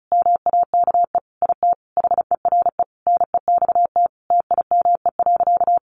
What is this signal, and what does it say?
Morse code, heard as a keyed tone, about 34 wpm, 700 Hz. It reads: MAKE IT HERE NEXT TIME.
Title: Sentences from Top 1500 Words 35wpm